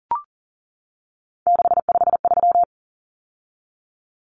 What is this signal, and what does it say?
Morse code, heard as a keyed tone, about 38 wpm, 700 Hz. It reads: 653
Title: Numbers - 3-Digit 40wpm